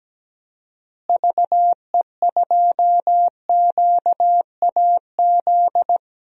Morse code, heard as a keyed tone, about 17 wpm, 700 Hz. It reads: VE2QAZ
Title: DXCC-100 Call Signs - 6 Characters - Repeated 2x - 17wpm